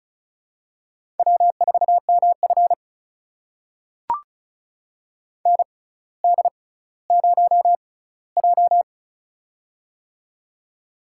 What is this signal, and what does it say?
Morse code, heard as a keyed tone, about 36 wpm, spaced as for 18 wpm, 700 Hz. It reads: W4MF ND0J
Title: USA Call Signs - 4 Characters - 6x Character Spacing - 35wpm